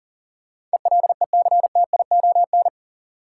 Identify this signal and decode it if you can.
Morse code, heard as a keyed tone, about 40 wpm, 700 Hz. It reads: ELECTION
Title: Sentences from Top 1500 Words - New Word Drill Alt - 40wpm